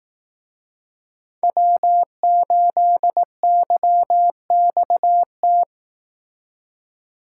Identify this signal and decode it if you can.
Morse code, heard as a keyed tone, about 18 wpm, 700 Hz. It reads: W8YXT
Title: Speed Racing - USA 1x3 Call Signs 15wpm